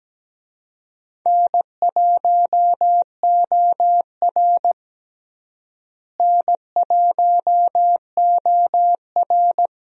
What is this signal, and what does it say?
Morse code, heard as a keyed tone, about 17 wpm, 700 Hz. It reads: N1OR N1OR
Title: USA Call Signs - 4 Characters - Repeated 2x - 17wpm